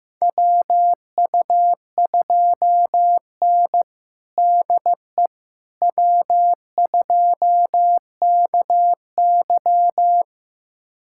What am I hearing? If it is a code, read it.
Morse code, heard as a keyed tone, about 15 wpm, 700 Hz. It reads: WU2N DE W2KY
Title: USA Callsign DE Callsign - 4 Characters - 15wpm